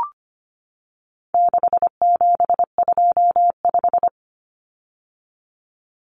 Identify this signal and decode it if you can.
Morse code, heard as a keyed tone, about 25 wpm, 700 Hz. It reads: 6725
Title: Numbers - 4-Digit 25wpm